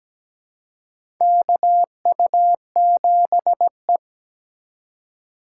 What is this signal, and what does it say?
Morse code, heard as a keyed tone, about 17 wpm, 700 Hz. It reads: KU7E
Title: USA Call Signs - 4 Characters - 4x Character Spacing - 17wpm